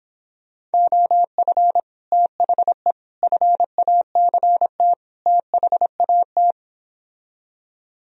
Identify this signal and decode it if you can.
Morse code, heard as a keyed tone, about 29 wpm, 700 Hz. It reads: OF THE FACT THAT